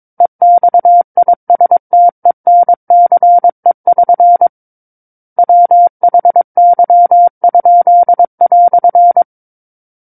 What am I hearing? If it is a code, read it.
Morse code, heard as a keyed tone, about 22 wpm, 700 Hz. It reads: EXISTENCE<SN> WHY?"
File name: Book One / 13-Chapter 13 - 22 WPM